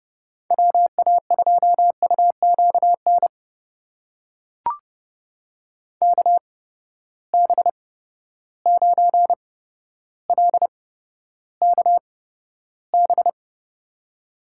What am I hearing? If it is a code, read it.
Morse code, heard as a keyed tone, about 30 wpm, spaced as for 13 wpm, 700 Hz. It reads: WA2UQN KB9LKB
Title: USA Call Signs - 6 Characters - 8x Character Spacing - 30wpm